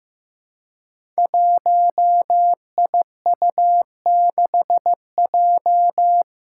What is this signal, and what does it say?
Morse code, heard as a keyed tone, about 15 wpm, 700 Hz. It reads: 1IU6J
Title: Five Challenging Letter-Number Combinations 15wpm